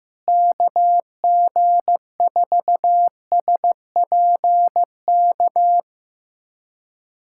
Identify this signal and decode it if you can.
Morse code, heard as a keyed tone, about 15 wpm, 700 Hz. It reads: KG4SPK
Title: USA Callsign DE Callsign 15wpm